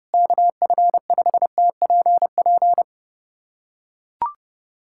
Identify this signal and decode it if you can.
Morse code, heard as a keyed tone, about 30 wpm, 700 Hz. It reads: KF5TPP
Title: USA Callsign DE Callsign - 6 Characters - 30wpm